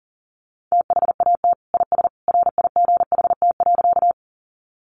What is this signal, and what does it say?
Morse code, heard as a keyed tone, about 40 wpm, 700 Hz. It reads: THAT IS RIGHT.
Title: Sentences from Top 200 Words 40wpm